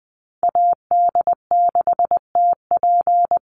Morse code, heard as a keyed tone, about 20 wpm, 700 Hz. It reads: AD6TP
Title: USA Call Signs Repeated 3x 20wpm